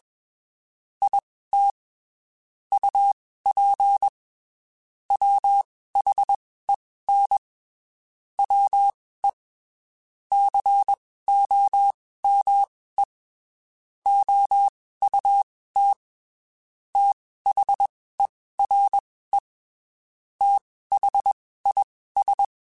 Morse code, heard as a keyed tone, about 21 wpm, spaced as for 15 wpm, 800 Hz. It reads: IT UP WHEN WE COME OUT THERE THIS
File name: (spaced) / QSOs Spaced-mp3s / WD0FBT @20 sp